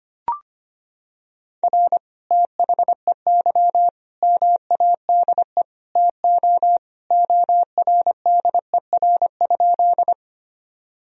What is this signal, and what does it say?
Morse code, heard as a keyed tone, about 25 wpm, 700 Hz. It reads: R THEY MADE TO ORDER?
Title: Sentences from Top 400 Words and Abbreviations 25wpm